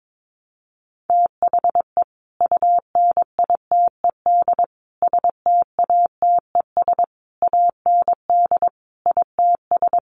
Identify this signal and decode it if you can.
Morse code, heard as a keyed tone, about 22 wpm, 700 Hz. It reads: THE UNITED STATES AND ITS
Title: Sets of 5 Words Encore 22wpm